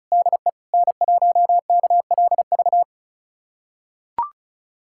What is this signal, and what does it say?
Morse code, heard as a keyed tone, about 36 wpm, 700 Hz. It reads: DE N1KLV
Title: USA Callsign DE Callsign 35wpm